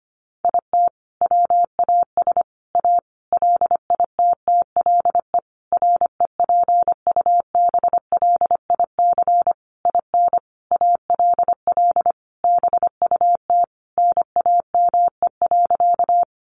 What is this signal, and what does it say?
Morse code, heard as a keyed tone, about 25 wpm, 700 Hz. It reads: IT WAS A LITTLE REPUBLIC IN ALL BUT NAME.
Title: Sentences from Top 1500 Words 25wpm